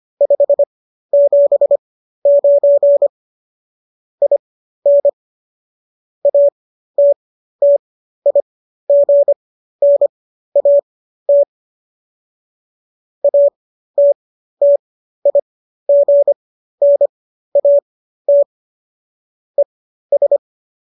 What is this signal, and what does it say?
Morse code, heard as a keyed tone, about 25 wpm, spaced as for 13 wpm, 570 Hz. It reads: 579 IN ATTIGNAT ATTIGNAT ES